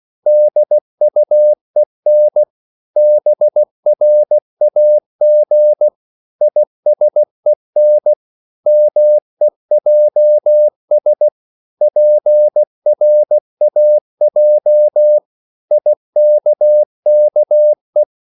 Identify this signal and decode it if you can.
Morse code, heard as a keyed tone, about 16 wpm, 600 Hz. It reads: DUEN BRAG ISEN MEJS PRAJ IKKE
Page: Korte danske ord 16wpm